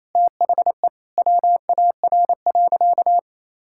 Morse code, heard as a keyed tone, about 28 wpm, 700 Hz. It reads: THE WAR.
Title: Sentences from Top 1000 Words - Review Part 6 - 28wpm